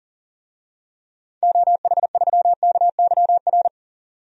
Morse code, heard as a keyed tone, about 38 wpm, 700 Hz. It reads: OH3KYR
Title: DXCC-100 Call Signs - 6 Characters - 40wpm